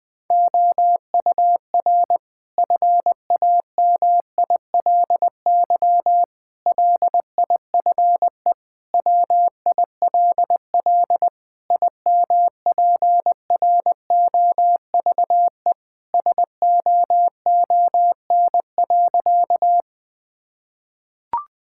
Morse code, heard as a keyed tone, about 21 wpm, 700 Hz. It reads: OUR FAMILY LIFE WILL IMPROVE SOON.
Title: Sentences from Top 1000 Words - Review Part 5 - 20wpm